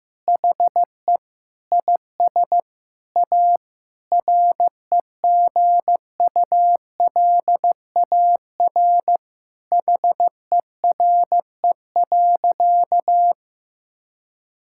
Morse code, heard as a keyed tone, about 15 wpm, 700 Hz. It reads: HE IS A REGULAR HERE.
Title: Sentences from Top 1000 Words - Review Part 6 - 15wpm